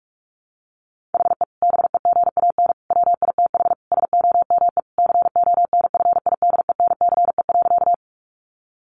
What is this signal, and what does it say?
Morse code, heard as a keyed tone, about 44 wpm, 700 Hz. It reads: HE BEGAN WITH SOME CONFIDENCE.
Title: Sentences from Top 1500 Words 45wpm